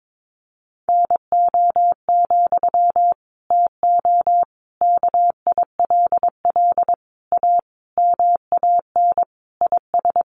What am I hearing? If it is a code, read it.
Morse code, heard as a keyed tone, about 22 wpm, 700 Hz. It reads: NO, TO KILL A MAN IS
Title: Sentences from Top 1400 Words 22wpm